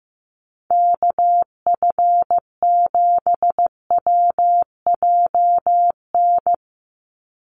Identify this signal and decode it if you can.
Morse code, heard as a keyed tone, about 15 wpm, 700 Hz. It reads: KF7WJN
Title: USA Call Signs 15wpm